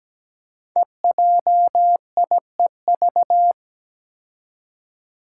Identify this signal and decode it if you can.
Morse code, heard as a keyed tone, about 17 wpm, 700 Hz. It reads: EJIEV
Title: Five Challenging Letter-Number Combinations 17wpm